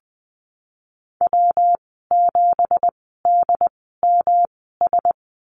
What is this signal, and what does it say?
Morse code, heard as a keyed tone, about 19 wpm, spaced as for 14 wpm, 700 Hz. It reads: W7DMS
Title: 173 Lesson 28g - Number 9 - Call Signs 20wpm